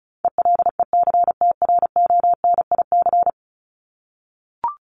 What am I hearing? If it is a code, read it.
Morse code, heard as a keyed tone, about 34 wpm, 700 Hz. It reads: ELECTRONIC
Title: Top 1001-1250 Words 35wpm